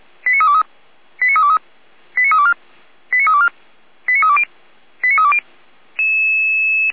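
Five-tone selective call (ZVEI), heard as quick runs of five tones, tones of about 80 ms. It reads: 89323 89323 89324 89324 89320 89320
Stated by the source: Five-tone ZVEI selective call received on an SDR, from the fire brigade Feuerwehr Orlamünde audio server.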